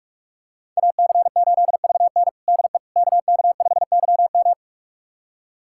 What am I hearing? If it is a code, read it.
Morse code, heard as a keyed tone, about 46 wpm, 700 Hz. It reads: AC8VN DE KK5YM
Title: USA Callsign DE Callsign - 5 Characters - 45wpm